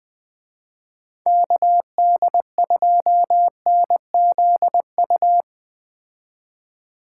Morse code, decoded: KD2NZU

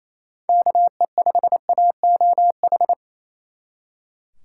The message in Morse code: KE5AOH

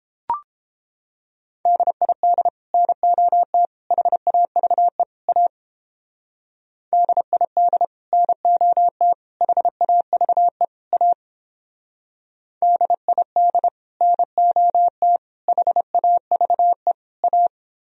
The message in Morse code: DID NOT HAVE A DID NOT HAVE A DID NOT HAVE A

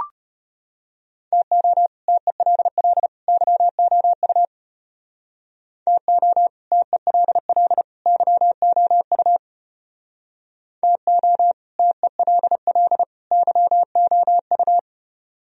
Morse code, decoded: TO TELL YOU TO TELL YOU TO TELL YOU

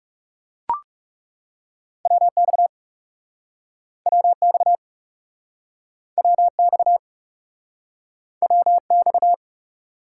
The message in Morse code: WX WX WX WX

